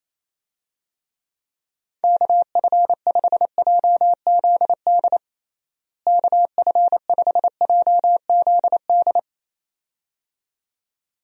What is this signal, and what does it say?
Morse code, heard as a keyed tone, about 28 wpm, 700 Hz. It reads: KF5JZD KF5JZD
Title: USA Call Signs - 6 Characters - Repeated 2x - 28wpm